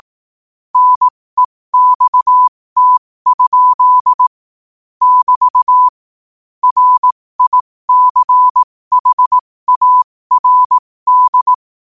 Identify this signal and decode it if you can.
Morse code, heard as a keyed tone, about 18 wpm, spaced as for 15 wpm, 1000 Hz. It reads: NEXT? <BT> RICHARD
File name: qotdfiles / 15wpm